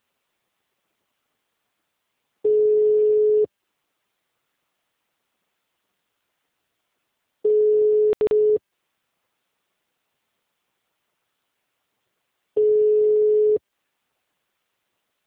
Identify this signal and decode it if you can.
Ringback tone (Europe, 425 Hz), with speech.